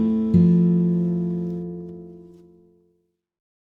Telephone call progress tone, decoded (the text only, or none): dial tone